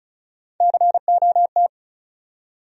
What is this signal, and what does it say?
Morse code, heard as a keyed tone, about 36 wpm, 700 Hz. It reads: COT
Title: USA Call Sign Postfixes - 3 Letters 35wpm